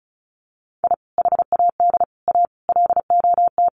Morse code, heard as a keyed tone, about 34 wpm, 700 Hz. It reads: I HAD A LOT
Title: Sets of 5 Words Encore 35wpm